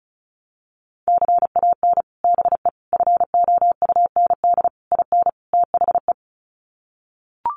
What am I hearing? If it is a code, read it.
Morse code, heard as a keyed tone, about 34 wpm, 700 Hz. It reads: CAN BE FOUND IN THE
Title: Sets of 5 Words Encore 35wpm